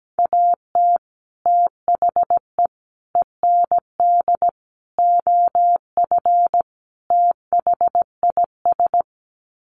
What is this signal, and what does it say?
Morse code, heard as a keyed tone, about 17 wpm, 700 Hz. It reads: AT THE END OF THIS
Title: Sets of 5 Words 17wpm